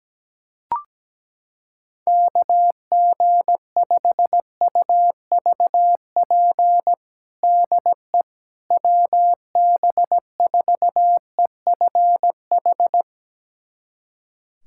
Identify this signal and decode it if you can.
Morse code, heard as a keyed tone, about 17 wpm, 700 Hz. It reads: KG5UVP DE WB4EFH
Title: USA Callsign DE Callsign 17wpm